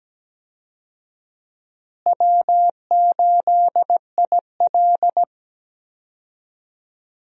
Morse code, heard as a keyed tone, about 17 wpm, 700 Hz. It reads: W8IL
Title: USA Call Signs - 4 Characters - Repeated 2x - 17wpm